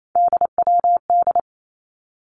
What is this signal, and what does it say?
Morse code, heard as a keyed tone, about 28 wpm, 700 Hz. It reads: DWD
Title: Three Challenging Letter-Number Combinations 28wpm